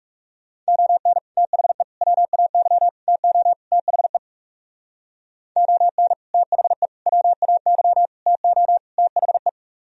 Morse code, heard as a keyed tone, about 36 wpm, 700 Hz. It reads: ON THE WAY TO THE ON THE WAY TO THE
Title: Speed-Racing - Sets of 5 Words 30wpm